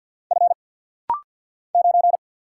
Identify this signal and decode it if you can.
Morse code, heard as a keyed tone, about 48 wpm, 700 Hz. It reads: F 9
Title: Single Letter-Number - Rapid-Fire 50wpm